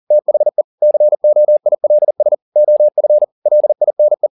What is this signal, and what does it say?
Morse code, heard as a keyed tone, about 40 wpm, 600 Hz. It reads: THE COILS OF LINE
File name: the old man and the sea 2 40wpm